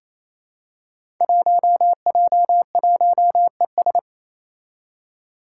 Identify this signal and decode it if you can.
Morse code, heard as a keyed tone, about 28 wpm, 700 Hz. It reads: 1J1ES
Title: Five Challenging Letter-Number Combinations 28wpm